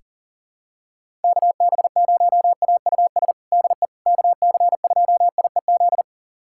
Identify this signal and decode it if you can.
Morse code, heard as a keyed tone, about 37 wpm, 700 Hz. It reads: KB0AUS DE KC2IEZ
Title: USA Callsign DE Callsign 40wpm